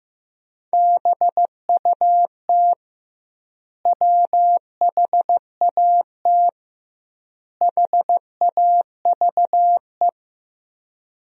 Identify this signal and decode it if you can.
Morse code, heard as a keyed tone, about 15 wpm, 700 Hz. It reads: BUT WHAT HAVE
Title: 2x Word-Spacing - Sentences from Top 100 Words - 15wpm